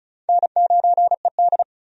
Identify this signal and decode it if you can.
Morse code, heard as a keyed tone, about 34 wpm, 700 Hz. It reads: N9ED
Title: USA Call Signs - 4 Characters - 35wpm